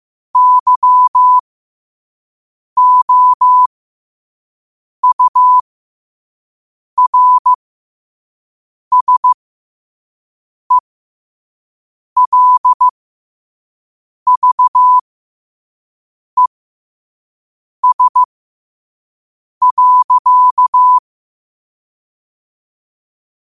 Morse code, heard as a keyed tone, about 15 wpm, spaced as for 5 wpm, 1000 Hz. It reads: YOURSELVES.
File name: qotdfiles / 5wpm